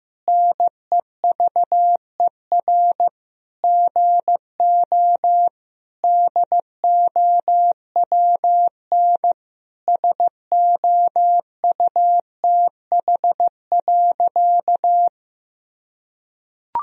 Morse code, heard as a keyed tone, about 15 wpm, 700 Hz. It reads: NEVER GO DOWN SOUTH.
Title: Sentences from Top 1000 Words - Review Part 2 - 15wpm